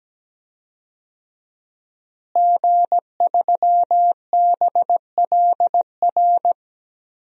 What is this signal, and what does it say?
Morse code, heard as a keyed tone, about 17 wpm, 700 Hz. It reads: G3BLR
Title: DXCC-100 Call Signs - 5 Characters - Repeated 3x - 17wpm